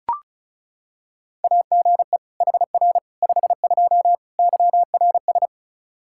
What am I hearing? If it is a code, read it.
Morse code, heard as a keyed tone, about 36 wpm, 700 Hz. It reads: AGE HR 52 YRS
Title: Practice QSO - Exchange 3 - More Info - 35wpm